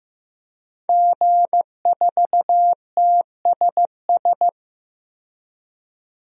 Morse code, heard as a keyed tone, about 15 wpm, 700 Hz. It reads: G4TSS
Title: DXCC-100 Call Signs - 5 Characters - 15wpm